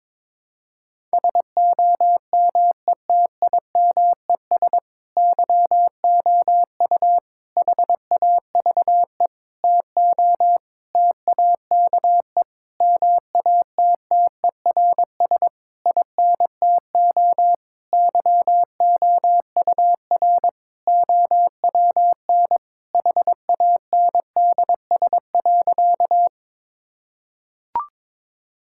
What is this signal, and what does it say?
Morse code, heard as a keyed tone, about 22 wpm, 700 Hz. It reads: SOMETIMES YOU HAVE TO TAKE MATTERS INTO YOUR OWN HANDS.
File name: Sentences with 7-Word Phrases - 22wpm